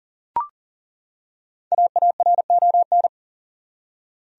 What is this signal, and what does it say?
Morse code, heard as a keyed tone, about 40 wpm, 700 Hz. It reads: AARON
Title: USA Common Names - Max 5 Letters - 40wpm